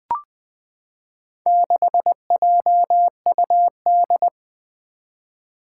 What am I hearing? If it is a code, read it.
Morse code, heard as a keyed tone, about 20 wpm, 700 Hz. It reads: 6JUD
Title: Four Challenging Letter-Number Combinations 20wpm